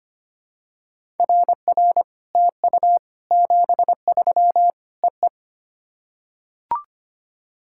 Morse code, heard as a keyed tone, about 25 wpm, 700 Hz. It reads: RR TU 73 EE
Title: SOTA QSO Practice 25wpm